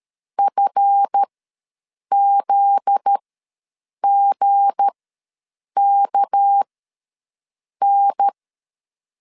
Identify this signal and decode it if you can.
Morse code, heard as a keyed tone, about 13 wpm, spaced as for 7 wpm, 800 Hz. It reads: FZGKN
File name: LEZIONI / 7 GF9Z / 7 GF9Z ws 12wpm 3321